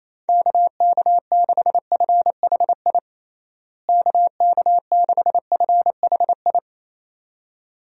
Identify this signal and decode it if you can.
Morse code, heard as a keyed tone, about 27 wpm, 700 Hz. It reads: KK6FHI KK6FHI
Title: USA Call Signs - 6 Characters - Repeated 2x - 28wpm